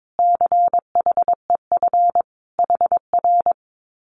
Morse code, heard as a keyed tone, about 22 wpm, 700 Hz. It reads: CHEF HR
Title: Common QSO Phrases Encore 22wpm